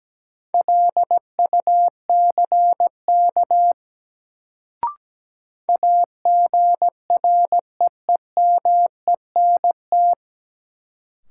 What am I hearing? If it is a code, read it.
Morse code, heard as a keyed tone, about 17 wpm, 700 Hz. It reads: LUCK AGREEMENT
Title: Sentences from Top 1500 Words - New Word Drill - 17wpm